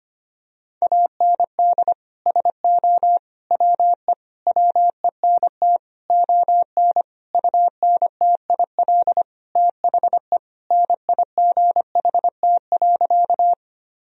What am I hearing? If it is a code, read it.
Morse code, heard as a keyed tone, about 25 wpm, 700 Hz. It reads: AND SO WE WENT ON UNTIL THE NIGHT.
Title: Sentences from Top 1000 Words - Review Part 1 - 25wpm